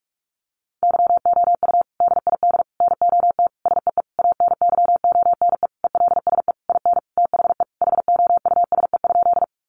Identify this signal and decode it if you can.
Morse code, heard as a keyed tone, about 44 wpm, 700 Hz. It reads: YOU DID NOT SEE ANYONE ELSE IN THE HOUSE?
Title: Sentences from Top 900 Words 45wpm